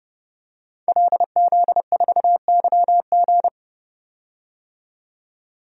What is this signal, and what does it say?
Morse code, heard as a keyed tone, about 31 wpm, 700 Hz. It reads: LZ4YG
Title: DXCC-100 Call Signs - 5 Characters - 30wpm